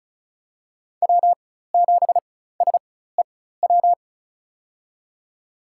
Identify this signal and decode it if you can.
Morse code, heard as a keyed tone, about 34 wpm, spaced as for 16 wpm, 700 Hz. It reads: W7SEW
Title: USA Call Signs - 5 Characters - 4x Character Spacing - 35wpm